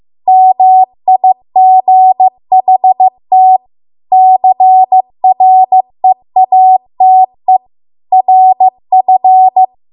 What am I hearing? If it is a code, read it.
Morse code, heard as a keyed tone, about 15 wpm, 750 Hz. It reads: MIGHT CREATE RF